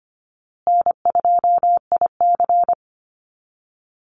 Morse code, heard as a keyed tone, about 25 wpm, 700 Hz. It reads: N2IC
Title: Call Signs Repeated 2x - 25wpm